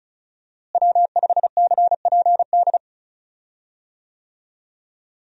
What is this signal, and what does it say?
Morse code, heard as a keyed tone, about 34 wpm, 700 Hz. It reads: W5CPD
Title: USA Call Signs - 5 Characters - Repeated 3x - 35wpm